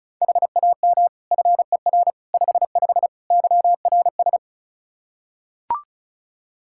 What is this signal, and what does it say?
Morse code, heard as a keyed tone, about 34 wpm, 700 Hz. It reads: HAM FER 55 YRS
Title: K1USN SST QSO Practice 35wpm